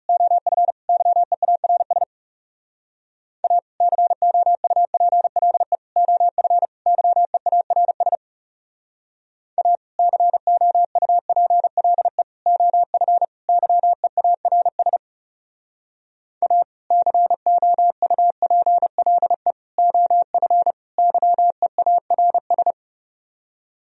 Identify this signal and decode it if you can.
Morse code, heard as a keyed tone, about 34 wpm, 700 Hz. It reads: OF YEARS A COUPLE OF YEARS A COUPLE OF YEARS A COUPLE OF YEARS
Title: Speed-Racing - Sets of 4 Words 30wpm